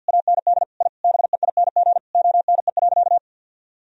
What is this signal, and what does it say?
Morse code, heard as a keyed tone, about 50 wpm, 700 Hz. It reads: AND I BEING ONE.
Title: Sentences from Top 1000 Words - Review Part 3 - 50wpm